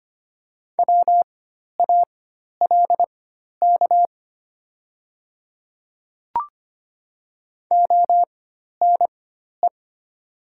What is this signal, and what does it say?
Morse code, heard as a keyed tone, about 25 wpm, spaced as for 12 wpm, 700 Hz. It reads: WALK ONE
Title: 4x Character Spacing - Top 300 Words - 25wpm